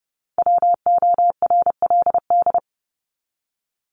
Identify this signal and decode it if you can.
Morse code, heard as a keyed tone, about 30 wpm, 700 Hz. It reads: WORLD
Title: Top 200 Words 30wpm